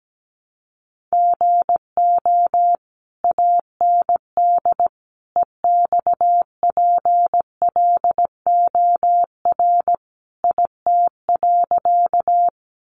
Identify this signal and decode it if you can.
Morse code, heard as a keyed tone, about 17 wpm, 700 Hz. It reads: GO AND EXPLOR IT.
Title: Sentences from Top 1100 Words 17wpm